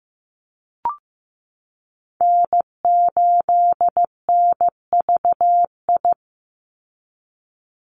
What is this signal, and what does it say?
Morse code, heard as a keyed tone, about 15 wpm, 700 Hz. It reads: N8NVI
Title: USA Call Signs 15wpm